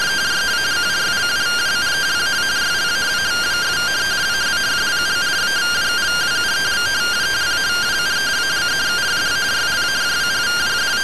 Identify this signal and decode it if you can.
Two-tone teleprinter signal, about 170 Hz shift.
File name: findings / MR CORONAL fr1 -131.gen-10 rtty rx